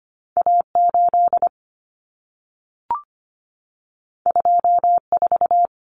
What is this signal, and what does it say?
Morse code, heard as a keyed tone, about 25 wpm, 700 Hz. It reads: A8 24
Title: Common Cut Numbers - 2-Digit 25wpm